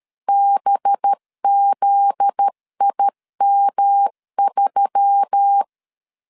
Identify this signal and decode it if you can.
Morse code, heard as a keyed tone, about 13 wpm, 800 Hz. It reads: BZIM3
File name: LEZIONI / 7 GF9Z / 7 GF9Z 12wpm 2147